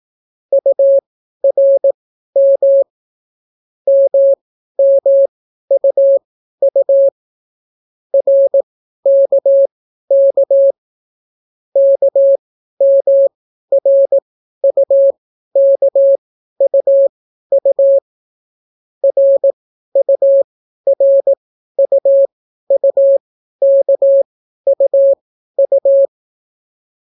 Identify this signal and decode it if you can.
Morse code, heard as a keyed tone, about 18 wpm, spaced as for 12 wpm, 550 Hz. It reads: URM MMUU RKK KMRUKUU RURUUKUU